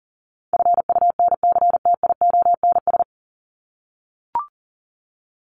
Morse code, heard as a keyed tone, about 40 wpm, 700 Hz. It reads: FUNCTIONS
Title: ES:S Suffix - Top 100 Words with Base Words 40wpm